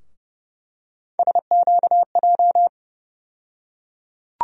Morse code, heard as a keyed tone, about 30 wpm, 700 Hz. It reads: SQJ
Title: USA Call Sign Postfixes - 3 Letters 30wpm